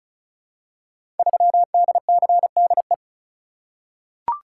Morse code, heard as a keyed tone, about 36 wpm, 700 Hz. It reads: 3DCDE
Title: Code Groups - 5 Letter:Numbers - 35wpm